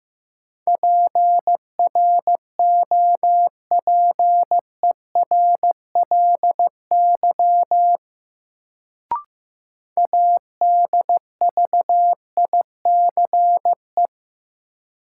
Morse code, heard as a keyed tone, about 15 wpm, 700 Hz. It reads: PROPERLY ADVICE
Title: Sentences from Top 1100 Words - New Word Drill - 15wpm